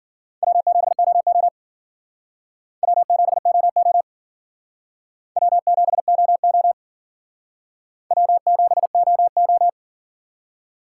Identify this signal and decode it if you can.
Morse code, heard as a keyed tone, about 50 wpm, 700 Hz. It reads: W7OO W7OO W7OO W7OO
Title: Speed Racing - USA 1x2 Call Signs 40wpm